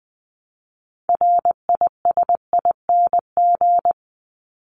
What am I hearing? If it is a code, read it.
Morse code, heard as a keyed tone, about 20 wpm, 700 Hz. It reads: RISING
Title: ING Suffix - Top 100 Words with Base Words 20wpm